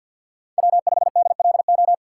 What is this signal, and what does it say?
Morse code, heard as a keyed tone, about 52 wpm, 700 Hz. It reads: W5DLO
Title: USA Call Signs - 5 Characters - 2x Character Spacing - 50wpm